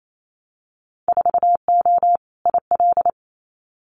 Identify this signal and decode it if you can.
Morse code, heard as a keyed tone, about 28 wpm, 700 Hz. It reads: 4O IL
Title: Winter Field Day Exchanges 28wpm